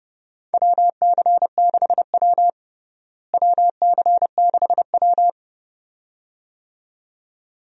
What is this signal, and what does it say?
Morse code, heard as a keyed tone, about 30 wpm, 700 Hz. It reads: WC6W WC6W
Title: USA Call Signs - 4 Characters - Repeated 2x - 30wpm